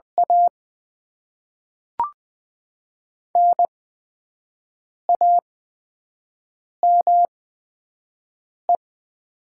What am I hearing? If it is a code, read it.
Morse code, heard as a keyed tone, about 20 wpm, spaced as for 5 wpm, 700 Hz. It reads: A NAME
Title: Top 20 QSO Elements - 8x Character Spacing - 20wpm